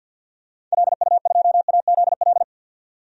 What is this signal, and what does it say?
Morse code, heard as a keyed tone, about 52 wpm, 700 Hz. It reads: LU2AZL
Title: DXCC-100 Call Signs - 6 Characters - 50wpm